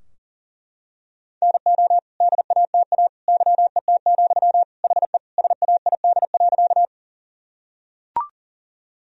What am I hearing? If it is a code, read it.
Morse code, heard as a keyed tone, about 40 wpm, 700 Hz. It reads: NO DATA YET, HE SAID.
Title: Sentences from Top 1000 Words - Review Part 3 - 40wpm